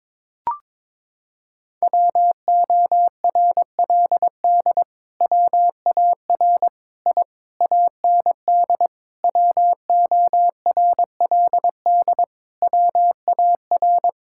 Morse code, heard as a keyed tone, about 22 wpm, 700 Hz. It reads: WORLD WAR I AND WORLD WAR
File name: Sentences with 7-Word Phrases - 22wpm